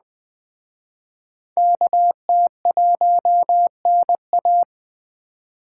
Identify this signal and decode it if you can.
Morse code, heard as a keyed tone, about 21 wpm, 700 Hz. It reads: KT1NA